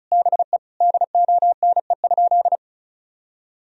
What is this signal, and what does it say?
Morse code, heard as a keyed tone, about 36 wpm, 700 Hz. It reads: BE DONE?
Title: Sentences from Top 1000 Words - Review Part 4 - 35wpm